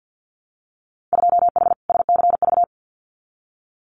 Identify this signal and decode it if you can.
Morse code, heard as a keyed tone, about 50 wpm, 700 Hz. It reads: 2H SCV
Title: Winter Field Day Exchanges Encore 50wpm